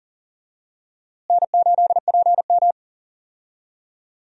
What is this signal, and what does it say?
Morse code, heard as a keyed tone, about 38 wpm, 700 Hz. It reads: N8PM
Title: USA Call Signs - 4 Characters - Repeated 2x - 40wpm